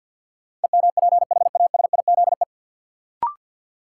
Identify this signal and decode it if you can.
Morse code, heard as a keyed tone, about 53 wpm, 700 Hz. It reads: EMPHASIZE
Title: Sentences from Top 1500 Words - New Word Drill - 50wpm